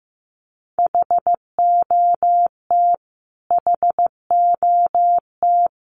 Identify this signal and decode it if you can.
Morse code, heard as a keyed tone, about 15 wpm, 700 Hz. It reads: HOT HOT
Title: Top 500 Words Repeated 15wpm